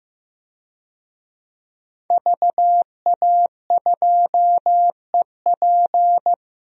Morse code, heard as a keyed tone, about 15 wpm, 700 Hz. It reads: VA2EP
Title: DXCC-100 Call Signs - 5 Characters - Repeated 2x - 15wpm